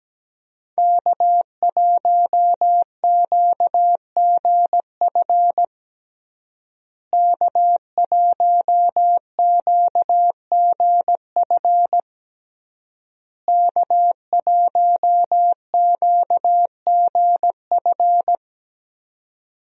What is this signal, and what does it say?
Morse code, heard as a keyed tone, about 17 wpm, 700 Hz. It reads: K1QGF K1QGF K1QGF
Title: USA Call Signs - 5 Characters - Repeated 3x - 17wpm